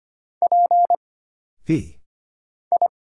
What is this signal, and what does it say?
Morse code, heard as a keyed tone, about 25 wpm, 700 Hz. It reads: P I
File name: Single Character (Letter:Number:Punctuation:Prosign) - Mind-Melt - 25wpm